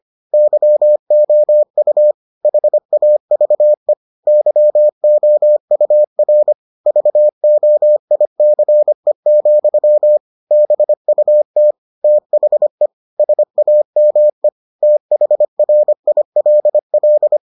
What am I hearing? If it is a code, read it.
Morse code, heard as a keyed tone, about 25 wpm, 600 Hz. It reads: YOU HAVE YOUR VOICE, BUT THE SAME THRILL